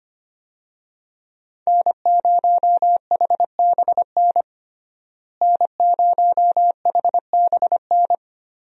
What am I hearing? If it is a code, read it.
Morse code, heard as a keyed tone, about 25 wpm, 700 Hz. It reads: N0HBN N0HBN
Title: USA Call Signs - 5 Characters - Repeated 3x - 25wpm